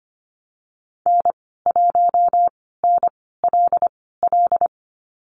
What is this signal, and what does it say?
Morse code, heard as a keyed tone, about 25 wpm, spaced as for 16 wpm, 700 Hz. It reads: N1NLL
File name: Lesson 15g - Number 5 - Call Signs 25wpm